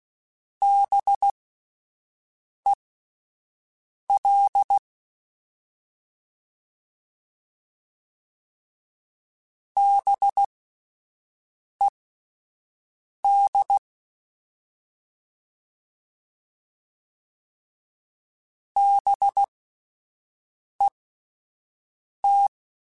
Morse code, heard as a keyed tone, about 16 wpm, spaced as for 5 wpm, 800 Hz. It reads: BEL BED BET
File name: Spaced Alphas / Spaced Alphas mp3s / Bs sp @10